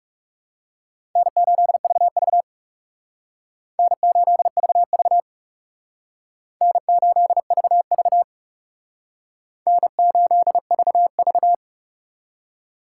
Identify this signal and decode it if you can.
Morse code, heard as a keyed tone, about 37 wpm, 700 Hz. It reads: N8VV N8VV N8VV N8VV
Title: Speed Racing - USA 1x2 Call Signs 30wpm